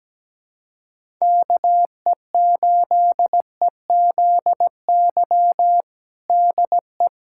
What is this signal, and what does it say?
Morse code, heard as a keyed tone, about 17 wpm, 700 Hz. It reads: KE8EZY DE
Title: USA Callsign DE Callsign - 6 Characters - 17wpm